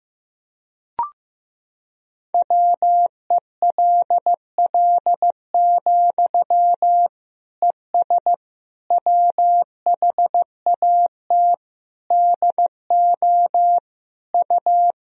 Morse code, heard as a keyed tone, about 15 wpm, 700 Hz. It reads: WELL, ES WHAT DO U
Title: Sentences from Top 200 Words and Abbreviations 15wpm